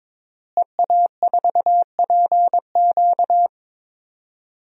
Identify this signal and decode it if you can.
Morse code, heard as a keyed tone, about 22 wpm, 700 Hz. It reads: EA4PQ
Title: DXCC-100 Call Signs - 5 Characters - Repeated 2x - 22wpm